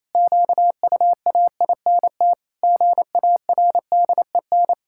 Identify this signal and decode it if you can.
Morse code, heard as a keyed tone, about 27 wpm, 700 Hz. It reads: QUAINT GARDEN
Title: Pangram Challenge v2 - 28wpm